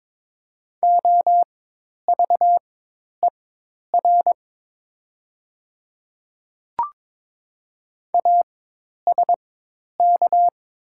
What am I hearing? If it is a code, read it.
Morse code, heard as a keyed tone, about 22 wpm, spaced as for 10 wpm, 700 Hz. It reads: OVER ASK
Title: 4x Character Spacing - Top 200 Words - 22wpm